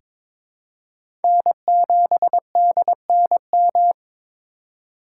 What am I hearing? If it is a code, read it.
Morse code, heard as a keyed tone, about 22 wpm, 700 Hz. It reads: N7DNM
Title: USA Call Signs - 5 Characters - 2x Character Spacing - 22wpm